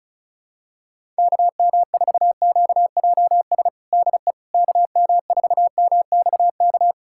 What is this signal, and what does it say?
Morse code, heard as a keyed tone, about 34 wpm, 700 Hz. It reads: KM4QJS DE KM4MXK
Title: USA Callsign DE Callsign - 6 Characters - 35wpm